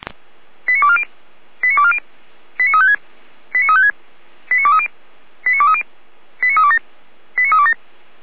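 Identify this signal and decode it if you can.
Five-tone selective call (ZVEI), heard as quick runs of five tones, tones of about 70 ms. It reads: 89240 89240 89356 89356 89230 89230 89327 89327
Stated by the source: ZVEI five-tone call 89130 received on an SDR.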